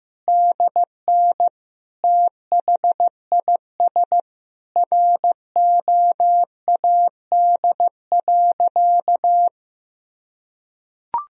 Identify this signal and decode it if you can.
Morse code, heard as a keyed tone, about 16 wpm, 700 Hz. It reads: DN THIS ROAD.
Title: Sentences from Top 400 Words and Abbreviations 15wpm